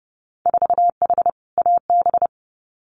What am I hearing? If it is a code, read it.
Morse code, heard as a keyed tone, about 30 wpm, 700 Hz. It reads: 4H AB
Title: Winter Field Day Exchanges Encore 30wpm